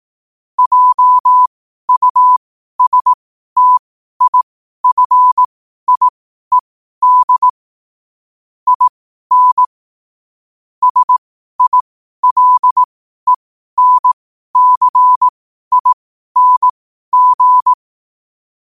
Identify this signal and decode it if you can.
Morse code, heard as a keyed tone, about 18 wpm, spaced as for 12 wpm, 1000 Hz. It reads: JUSTIFIED IN SILENCING
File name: qotdfiles / 12wpm